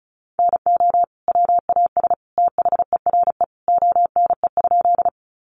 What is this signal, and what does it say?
Morse code, heard as a keyed tone, about 34 wpm, 700 Hz. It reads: NO WAS THERE ONE?
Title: Sentences from Top 100 Words 35wpm